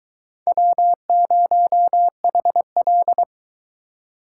Morse code, heard as a keyed tone, about 22 wpm, 700 Hz. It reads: W0HL